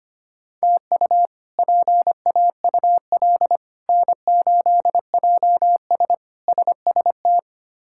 Morse code, heard as a keyed tone, about 25 wpm, 700 Hz. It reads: TU PAUL N8JS SST
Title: K1USN SST QSO Practice - 25wpm